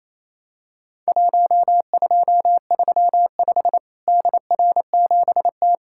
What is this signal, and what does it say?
Morse code, heard as a keyed tone, about 27 wpm, 700 Hz. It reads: 1235 DR7T
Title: Worked All Europe DX Contest - QTC Exchange - 28wpm